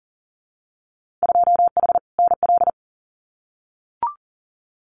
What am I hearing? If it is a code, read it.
Morse code, heard as a keyed tone, about 40 wpm, 700 Hz. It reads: 2H NL
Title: Winter Field Day Exchanges Encore 40wpm